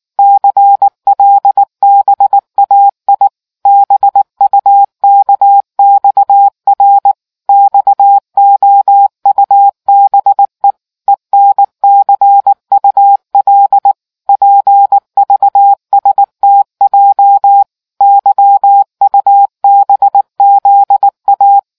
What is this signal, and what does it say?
Morse code, heard as a keyed tone, about 19 wpm, 800 Hz. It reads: CLBAI BUKXR XOUBE ENCUL PVSTJ YUBZA